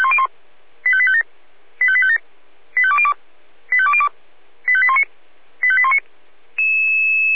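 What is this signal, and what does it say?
Five-tone selective call (ZVEI), heard as quick runs of five tones, tones of about 70 ms. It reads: 85857 85857 85202 85202 85819 85819
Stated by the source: ZVEI five-tone selective call received on an SDR.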